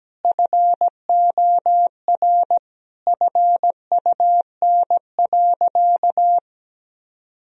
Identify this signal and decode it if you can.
Morse code, heard as a keyed tone, about 17 wpm, 700 Hz. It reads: FOR FUN.
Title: Sentences from Top 1000 Words - Review Part 3 - 17wpm